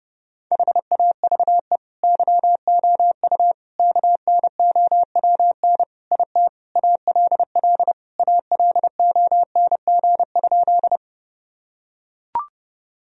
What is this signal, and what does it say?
Morse code, heard as a keyed tone, about 30 wpm, 700 Hz. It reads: HAVE YOU KNOWN IT ALL ALONG?
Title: Sentences from Top 1000 Words - Review Part 3 - 30wpm